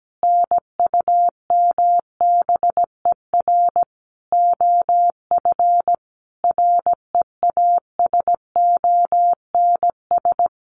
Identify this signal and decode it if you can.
Morse code, heard as a keyed tone, about 17 wpm, 700 Hz. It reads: NUMBER OF REASONS
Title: Sets of 5 Words Encore2 17wpm